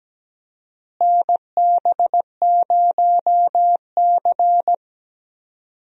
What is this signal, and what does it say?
Morse code, heard as a keyed tone, about 17 wpm, 700 Hz. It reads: NB0C
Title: USA Call Signs - 4 Characters - Repeated 3x - 17wpm